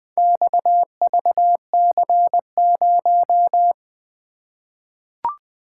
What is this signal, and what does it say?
Morse code, heard as a keyed tone, about 20 wpm, 700 Hz. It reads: XVC0
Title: Four Letter-Number Combinations - 20wpm